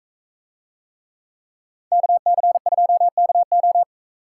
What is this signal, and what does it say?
Morse code, heard as a keyed tone, about 40 wpm, 700 Hz. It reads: KC2KO